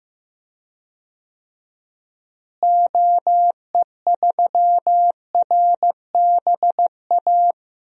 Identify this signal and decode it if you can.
Morse code, heard as a keyed tone, about 15 wpm, 700 Hz. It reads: OE3RBA
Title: DXCC-100 Call Signs - 6 Characters - Repeated 3x - 15wpm